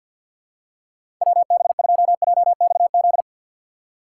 Morse code, heard as a keyed tone, about 47 wpm, 700 Hz. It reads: WB2JXZ
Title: USA Call Signs - 6 Characters - 3x Character Spacing - 50wpm